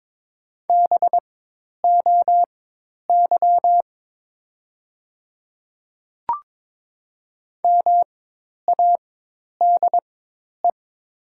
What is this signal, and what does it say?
Morse code, heard as a keyed tone, about 22 wpm, spaced as for 10 wpm, 700 Hz. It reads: BOY MADE
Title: 4x Character Spacing - Top 200 Words - 22wpm